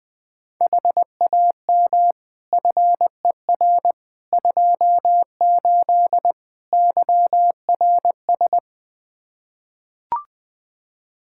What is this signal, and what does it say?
Morse code, heard as a keyed tone, about 21 wpm, 700 Hz. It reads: HAM FER 28 YRS
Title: Practice QSO - Exchange 3 - More Info - 20wpm